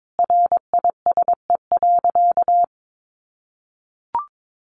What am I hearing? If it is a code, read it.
Morse code, heard as a keyed tone, about 22 wpm, 700 Hz. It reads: RISE.
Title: Sentences from Top 1300 Words 22wpm